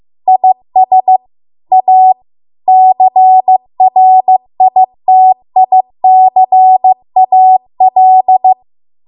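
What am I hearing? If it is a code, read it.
Morse code, heard as a keyed tone, about 15 wpm, 750 Hz. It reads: IS A CRITICAL
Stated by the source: Morse code practice at 15 wpm from the national amateur radio association's code archive.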